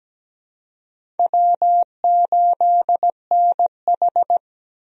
Morse code, heard as a keyed tone, about 17 wpm, 700 Hz. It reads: W8NH
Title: USA Call Signs - 4 Characters - 17wpm